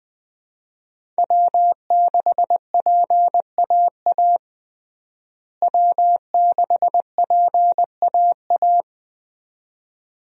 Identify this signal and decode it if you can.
Morse code, heard as a keyed tone, about 20 wpm, 700 Hz. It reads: W6PAA W6PAA
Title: USA Call Signs - 5 Characters - Repeated 2x - 20wpm